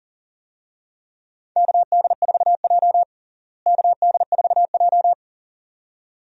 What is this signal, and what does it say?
Morse code, heard as a keyed tone, about 40 wpm, 700 Hz. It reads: KD4J KD4J
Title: USA Call Signs - 4 Characters - Repeated 2x - 40wpm